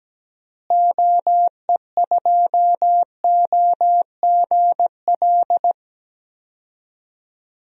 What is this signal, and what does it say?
Morse code, heard as a keyed tone, about 17 wpm, 700 Hz. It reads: OE2OGL
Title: DXCC-100 Call Signs - 6 Characters - Repeated 3x - 17wpm